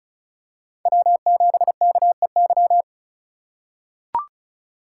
Morse code, heard as a keyed tone, about 36 wpm, 700 Hz. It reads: W7KEY
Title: USA Call Signs - 4x Character Spacing - 35wpm